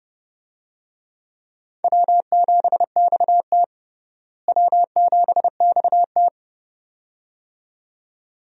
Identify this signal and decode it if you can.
Morse code, heard as a keyed tone, about 30 wpm, 700 Hz. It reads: W7XT W7XT
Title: USA Call Signs - 4 Characters - Repeated 2x - 30wpm